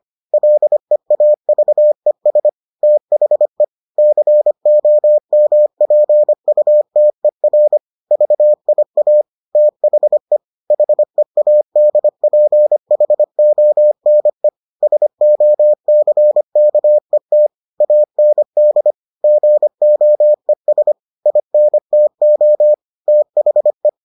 Morse code, heard as a keyed tone, about 25 wpm, 600 Hz. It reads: LEAVES THE COMPUTER VIA THE HEADPHONE SOCKET AND GOES INTO THE